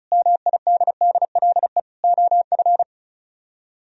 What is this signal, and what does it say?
Morse code, heard as a keyed tone, about 42 wpm, 700 Hz. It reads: MIDDLE OF
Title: Speed-Racing - Sets of 5 Words 35wpm